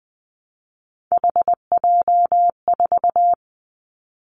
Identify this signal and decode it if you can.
Morse code, heard as a keyed tone, about 19 wpm, 700 Hz. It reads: HJ4
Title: Callsign Prefixes for Top 100 DXCC - 20wpm